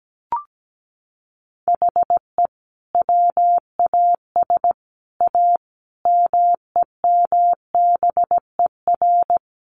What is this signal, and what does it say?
Morse code, heard as a keyed tone, about 17 wpm, 700 Hz. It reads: HE WAS A MEMBER
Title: Sets of 5 Words Encore2 17wpm